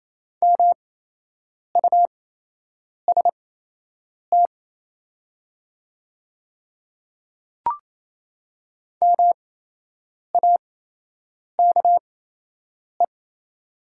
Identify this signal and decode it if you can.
Morse code, heard as a keyed tone, about 28 wpm, spaced as for 8 wpm, 700 Hz. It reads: MUST MAKE
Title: 8x Character Spacing - Top 200 Words - 28wpm